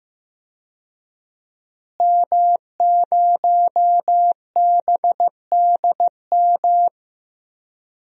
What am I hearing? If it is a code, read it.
Morse code, heard as a keyed tone, about 15 wpm, 700 Hz. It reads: M0BDM
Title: DXCC-100 Call Signs - 5 Characters - Repeated 3x - 15wpm